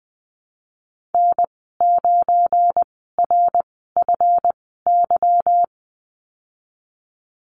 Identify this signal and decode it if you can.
Morse code, heard as a keyed tone, about 20 wpm, spaced as for 15 wpm, 700 Hz. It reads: N9RFY
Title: 186 Lesson 30g - Letter b - Call Signs 20wpm